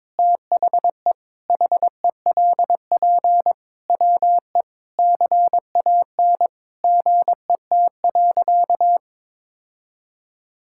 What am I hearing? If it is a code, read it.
Morse code, heard as a keyed tone, about 22 wpm, 700 Hz. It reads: THE HELP WE CAN GET.
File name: practice / 22wpm / Sentences with 8-Word Phrases - 22wpm